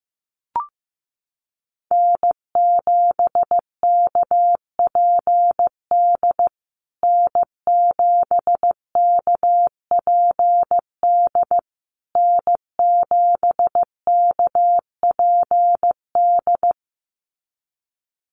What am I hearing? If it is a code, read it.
Morse code, heard as a keyed tone, about 15 wpm, 700 Hz. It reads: N7KPD N7KPD N7KPD
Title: USA Call Signs Repeated 3x 15wpm